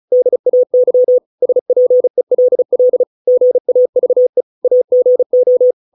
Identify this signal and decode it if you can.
Morse code, heard as a keyed tone, about 35 wpm, 500 Hz. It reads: DAY SPELL GAVE AGO